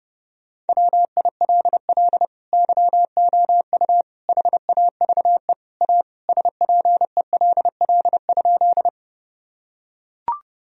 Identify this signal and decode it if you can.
Morse code, heard as a keyed tone, about 30 wpm, 700 Hz. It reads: WILL YOU HAVE A SPELL?
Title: Sentences from Top 1000 Words - Review Part 1 - 30wpm